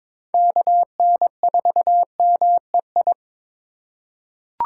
Morse code, heard as a keyed tone, about 22 wpm, 700 Hz. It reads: KN4MEI
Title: USA Call Signs - 6 Characters - 2x Character Spacing - 22wpm